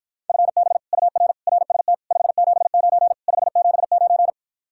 Morse code, heard as a keyed tone, about 54 wpm, 700 Hz. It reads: FB UR RST 579 579